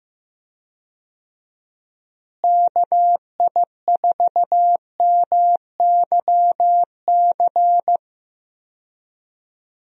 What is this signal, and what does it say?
Morse code, heard as a keyed tone, about 15 wpm, 700 Hz. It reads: KI4MYC
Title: USA Call Signs - 6 Characters - Repeated 3x - 15wpm